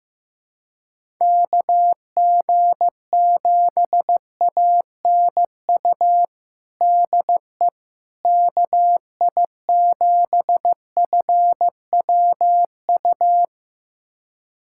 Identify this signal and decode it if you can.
Morse code, heard as a keyed tone, about 15 wpm, 700 Hz. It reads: KG7ANU DE KI7FWU
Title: USA Callsign DE Callsign - 6 Characters - 15wpm